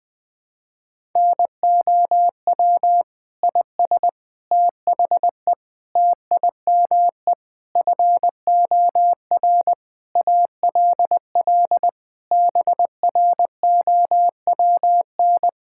Morse code, heard as a keyed tone, about 20 wpm, 700 Hz. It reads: NOW IS THE TIME FOR ALL BROWN
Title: Pangram Challenge v2 - 20wpm